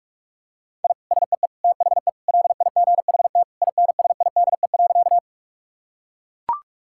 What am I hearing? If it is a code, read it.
Morse code, heard as a keyed tone, about 43 wpm, 700 Hz. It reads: I SEE THE LIGHT INSIDE.
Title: Sentences from Top 1000 Words - Review Part 6 - 45wpm